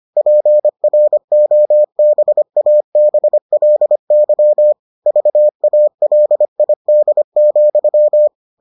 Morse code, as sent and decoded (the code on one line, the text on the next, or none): .--. .-. --- -... .- -... .-.. -.-- / ...- .- .-.. .. -.. --..--
PROBABLY VALID,